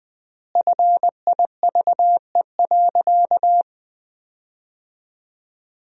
Morse code, heard as a keyed tone, about 20 wpm, 700 Hz. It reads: FIVE.
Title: 3x Word-Spacing - Sentences from Top 400 Words - 20wpm